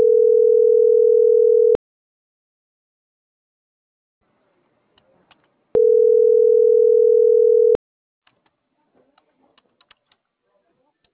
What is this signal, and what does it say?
Ringback tone (North America), 440 + 480 Hz.